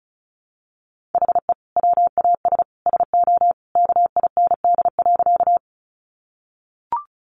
Morse code, heard as a keyed tone, about 34 wpm, 700 Hz. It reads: HE WAS SO KIND.
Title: Sentences from Top 200 Words 35wpm